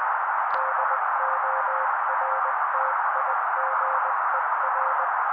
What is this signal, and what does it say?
Morse code, heard as a keyed tone, about 20 wpm, 530 Hz. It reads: DORTIGER
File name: daily / 18wpm noise